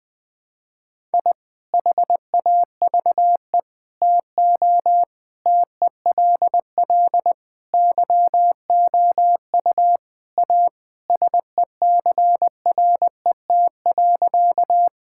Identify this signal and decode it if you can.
Morse code, heard as a keyed tone, about 20 wpm, 700 Hz. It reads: I HAVE TO TELL YOU A SECRET.
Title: Sentences with 5-Word Phrases - Spoken Beginning - 20wpm